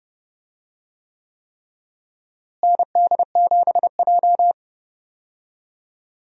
Morse code, heard as a keyed tone, about 30 wpm, 700 Hz. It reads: ND7J
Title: USA Call Signs - 4 Characters - Repeated 3x - 30wpm